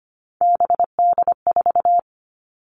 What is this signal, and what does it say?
Morse code, heard as a keyed tone, about 25 wpm, 700 Hz. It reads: BD4
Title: All Cut Numbers - 3-Digit 25wpm